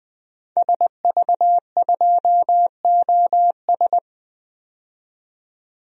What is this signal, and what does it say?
Morse code, heard as a keyed tone, about 20 wpm, 700 Hz. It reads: SV2OS
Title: DXCC-100 Call Signs - 5 Characters - 20wpm